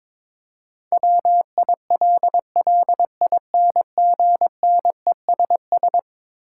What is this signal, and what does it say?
Morse code, heard as a keyed tone, about 22 wpm, 700 Hz. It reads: WILLINGNESS
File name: practice / 22wpm / Eleven Letter Words - 22wpm